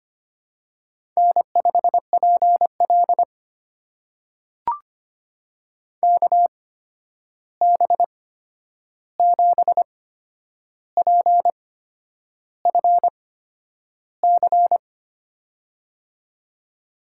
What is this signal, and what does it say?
Morse code, heard as a keyed tone, about 25 wpm, spaced as for 9 wpm, 700 Hz. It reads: N5PL KB7PFC
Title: USA Call Signs - 8x Character Spacing - 25wpm